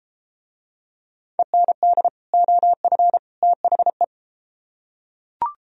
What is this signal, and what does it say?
Morse code, heard as a keyed tone, about 33 wpm, 700 Hz. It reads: END OF THE